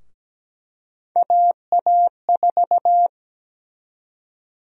Morse code, heard as a keyed tone, about 17 wpm, 700 Hz. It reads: AA4
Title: USA Call Sign Prefixes - Repeated 2x - 17wpm